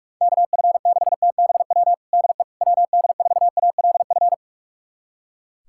Morse code, heard as a keyed tone, about 42 wpm, 700 Hz. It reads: KF6TBW DE WD4ALF
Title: USA Callsign DE Callsign - 6 Characters - 45wpm